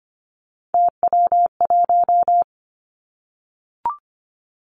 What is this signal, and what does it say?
Morse code, heard as a keyed tone, about 25 wpm, 700 Hz. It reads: TW1
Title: Three Challenging Letter-Number Combinations 25wpm